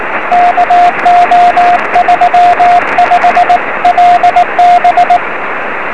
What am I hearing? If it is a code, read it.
Morse code, heard as a keyed tone, about 19 wpm, 710 Hz, with radio noise. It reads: KO35LB
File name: 0559z ew6fs 23cm